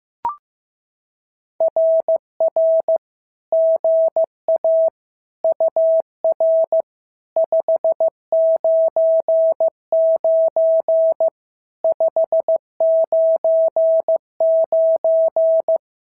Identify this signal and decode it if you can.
Morse code, heard as a keyed tone, about 14 wpm, 650 Hz. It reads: RR GA UR 599 599
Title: SOTA QSO Practice 15wpm